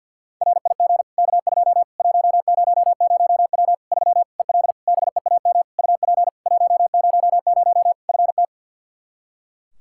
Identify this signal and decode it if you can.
Morse code, heard as a keyed tone, about 45 wpm, 700 Hz. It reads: RIG K2 100W 3 EL BEAM UP 100 FT
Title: K1USN SST QSO Practice 50wpm